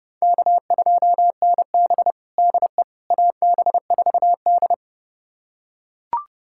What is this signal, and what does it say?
Morse code, heard as a keyed tone, about 30 wpm, 700 Hz. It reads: K2NB DE AB4D
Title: USA Callsign DE Callsign - 4 Characters - 30wpm